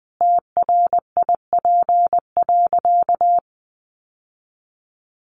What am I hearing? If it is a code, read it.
Morse code, heard as a keyed tone, about 20 wpm, 700 Hz. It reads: TRIP.
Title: Sentences from Top 1300 Words 20wpm